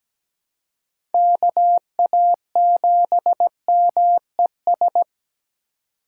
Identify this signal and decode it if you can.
Morse code, heard as a keyed tone, about 17 wpm, 700 Hz. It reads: KA7MES
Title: USA Call Signs - 6 Characters - 8x Character Spacing - 17wpm